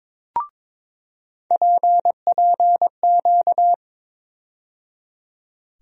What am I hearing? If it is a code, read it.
Morse code, heard as a keyed tone, about 21 wpm, 700 Hz. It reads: PPQ
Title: USA Call Sign Postfixes - 3 Letters 22wpm